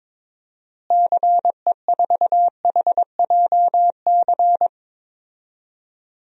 Morse code, heard as a keyed tone, about 22 wpm, 700 Hz. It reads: CE4HJC
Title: DXCC-100 Call Signs - 6 Characters - Repeated 2x - 22wpm